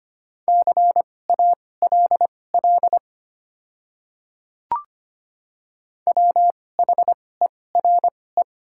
Morse code, heard as a keyed tone, about 25 wpm, spaced as for 18 wpm, 700 Hz. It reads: CALL WHERE
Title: 2x Character Spacing - Top 200 Words - 25wpm